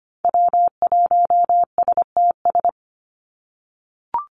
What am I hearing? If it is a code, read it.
Morse code, heard as a keyed tone, about 25 wpm, 700 Hz. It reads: W1STS
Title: USA Call Signs 25wpm